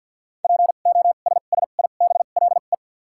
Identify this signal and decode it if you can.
Morse code, heard as a keyed tone, about 53 wpm, 700 Hz. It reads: P O S S I B L E
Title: 2x Character Spacing - Top 500 Words - 50wpm